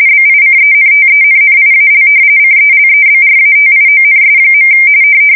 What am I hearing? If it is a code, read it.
RTTY, about 45.45 baud. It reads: BICENTENNIAL CELEBRATIONS.
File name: fsk170 / battleshipnj